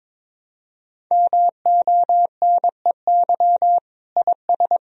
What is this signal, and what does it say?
Morse code, heard as a keyed tone, about 22 wpm, 700 Hz. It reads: MONEY IS
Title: Sentences from Top 500 Words and Abbreviations 22wpm